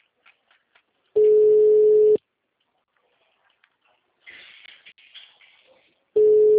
Ringback tone (Europe, 425 Hz).
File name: out- -2019